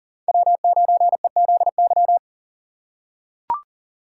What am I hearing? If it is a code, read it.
Morse code, heard as a keyed tone, about 40 wpm, 700 Hz. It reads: W9EZY
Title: USA Call Signs - 5 Characters - 6x Character Spacing - 40wpm